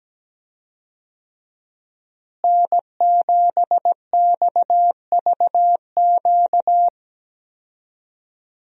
Morse code, heard as a keyed tone, about 17 wpm, 700 Hz. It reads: N7XVQ
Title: USA Call Signs - 5 Characters - Repeated 2x - 17wpm